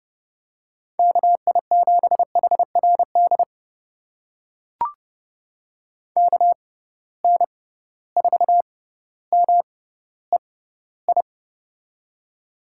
Morse code, heard as a keyed tone, about 32 wpm, spaced as for 16 wpm, 700 Hz. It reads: KI7HRD KN4MEI
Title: USA Call Signs - 6 Characters - 6x Character Spacing - 30wpm